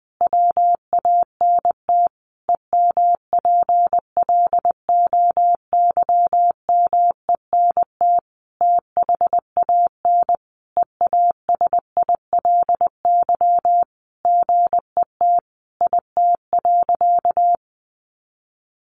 Morse code, heard as a keyed tone, about 20 wpm, 700 Hz. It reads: WANT EMPLOYMENT THAN EASILY GET IT.
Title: Sentences from Top 1500 Words 20wpm